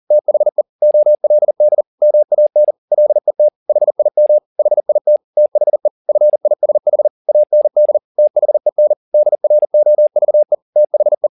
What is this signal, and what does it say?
Morse code, heard as a keyed tone, about 40 wpm, 600 Hz. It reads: THE OLD MAN LET HIM HIT THE FISH AND THEN DROVE THE
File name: the old man and the sea 2 40wpm